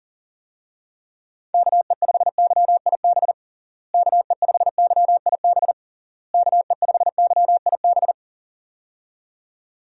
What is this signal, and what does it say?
Morse code, heard as a keyed tone, about 40 wpm, 700 Hz. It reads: KE5YIB KE5YIB KE5YIB
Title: USA Call Signs - 6 Characters - Repeated 3x - 40wpm